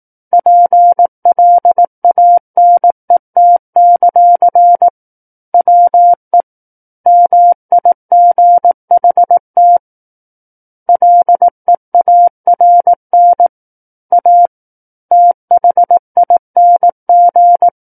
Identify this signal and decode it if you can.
Morse code, heard as a keyed tone, about 18 wpm, 700 Hz. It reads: PLANET; WE MIGHT LEARN A THING
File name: 18 WPM